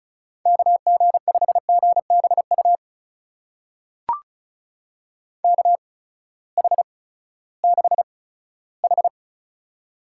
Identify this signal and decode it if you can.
Morse code, heard as a keyed tone, about 36 wpm, spaced as for 18 wpm, 700 Hz. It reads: KG5GBU KH6H
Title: USA Call Signs - 8x Character Spacing - 35wpm